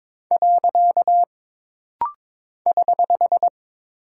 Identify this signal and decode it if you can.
Morse code, heard as a keyed tone, about 22 wpm, 700 Hz. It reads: . <HH>
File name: practice / 22wpm / Single Punctuation-Prosign - Rapid-Fire - 22wpm